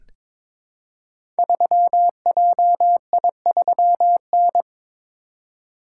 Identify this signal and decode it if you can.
Morse code, heard as a keyed tone, about 22 wpm, 700 Hz. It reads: 3JI3N
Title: Code Groups - 5 Letter:Numbers - 22wpm